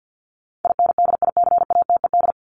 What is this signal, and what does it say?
Morse code, heard as a keyed tone, about 50 wpm, 700 Hz. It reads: INDICATED
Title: ED Suffix - Top 100 Words with Base Words 50wpm